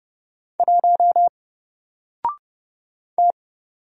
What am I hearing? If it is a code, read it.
Morse code, heard as a keyed tone, about 30 wpm, 700 Hz. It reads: 1 T
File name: Single-Letter-Number-30wpm